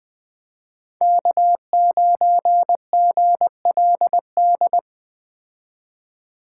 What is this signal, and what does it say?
Morse code, heard as a keyed tone, about 20 wpm, 700 Hz. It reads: K9GLD